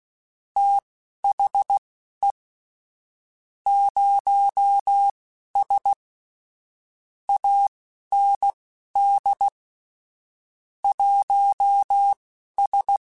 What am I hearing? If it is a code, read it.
Morse code, heard as a keyed tone, about 16 wpm, spaced as for 11 wpm, 800 Hz. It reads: THE 0S AND 1S
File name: Technical Subjects / Technical Subjects-mp3s / Gates Intro @10